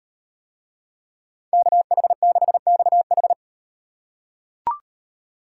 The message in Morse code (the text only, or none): KH6XH